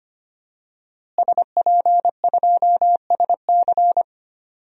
Morse code, heard as a keyed tone, about 25 wpm, 700 Hz. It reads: SP2SC
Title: DXCC-100 Call Signs - 5 Characters - Repeated 3x - 25wpm